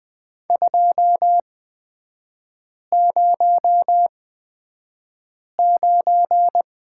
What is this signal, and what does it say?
Morse code, heard as a keyed tone, about 20 wpm, spaced as for 5 wpm, 700 Hz. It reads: 209
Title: 250 Lesson 40e - BK - Three Characters 20wpm